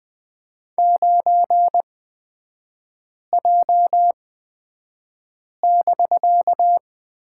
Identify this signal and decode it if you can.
Morse code, heard as a keyed tone, about 20 wpm, spaced as for 5 wpm, 700 Hz. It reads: 9J<BK>
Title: 250 Lesson 40e - BK - Three Characters 20wpm